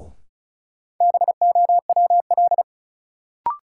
Morse code, heard as a keyed tone, about 36 wpm, 700 Hz. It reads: BOWL
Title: Sentences from Top 1200 Words - New Word Drill Alt - 35wpm